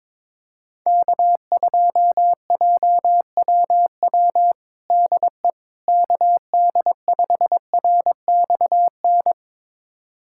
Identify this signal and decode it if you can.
Morse code, heard as a keyed tone, about 21 wpm, 700 Hz. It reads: K2JWW DE KD5RXN
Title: USA Callsign DE Callsign 22wpm